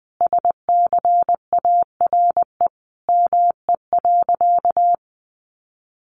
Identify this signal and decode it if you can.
Morse code, heard as a keyed tone, about 20 wpm, 700 Hz. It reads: SCARE ME.
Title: A Sound Of Thunder 20wpm